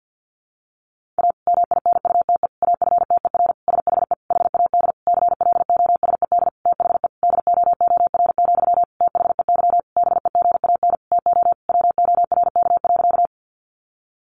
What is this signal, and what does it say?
Morse code, heard as a keyed tone, about 50 wpm, 700 Hz. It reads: A MINUTE AFTER SHE HAD CLOSED THE DOOR, THEY BEGAN TO WORK.
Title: Sentences from Top 1400 Words 50wpm